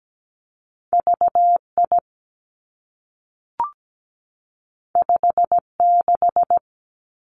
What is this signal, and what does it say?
Morse code, heard as a keyed tone, about 17 wpm, 700 Hz. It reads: VI 56
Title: Two Challenging Letter-Number Combinations 17wpm